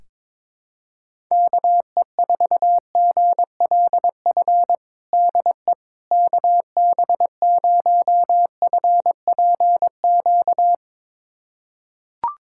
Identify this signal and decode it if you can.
Morse code, heard as a keyed tone, about 21 wpm, 700 Hz. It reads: KE4GLF DE KB0FPQ
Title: USA Callsign DE Callsign 22wpm